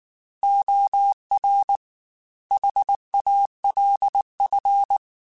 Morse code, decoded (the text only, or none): OR HALF